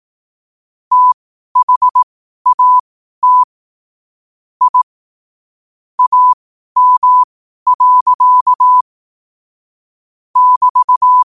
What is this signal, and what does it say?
Morse code, heard as a keyed tone, about 18 wpm, spaced as for 12 wpm, 1000 Hz. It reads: THAT I AM. <BT>
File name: qotdfiles / 12wpm